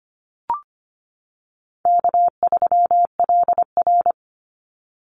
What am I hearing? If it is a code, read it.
Morse code, heard as a keyed tone, about 25 wpm, 700 Hz. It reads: K3LR
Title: Easy Call Signs - 25wpm